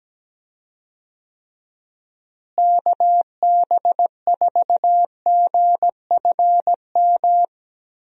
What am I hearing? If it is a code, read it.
Morse code, heard as a keyed tone, about 17 wpm, 700 Hz. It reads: KB4GFM
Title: USA Call Signs - 6 Characters - Repeated 2x - 17wpm